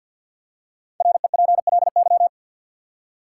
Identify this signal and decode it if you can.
Morse code, heard as a keyed tone, about 53 wpm, 700 Hz. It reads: REPLY
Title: Five Letter Words - 50wpm